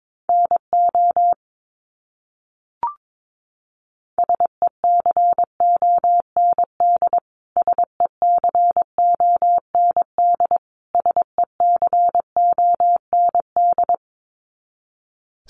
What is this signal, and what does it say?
Morse code, heard as a keyed tone, about 22 wpm, 700 Hz. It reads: NO SECOND SECOND SECOND
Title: Top 500 Words Repeated 22wpm